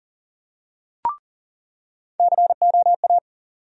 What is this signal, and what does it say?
Morse code, heard as a keyed tone, about 40 wpm, 700 Hz. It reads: COA
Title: USA Call Sign Postfixes - 3 Letters 40wpm